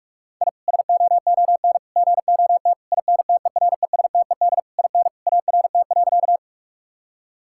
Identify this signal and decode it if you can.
Morse code, heard as a keyed tone, about 46 wpm, 700 Hz. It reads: I SOON GOT INTERESTED IN ART.
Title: Sentences from Top 1000 Words - Review Part 6 - 45wpm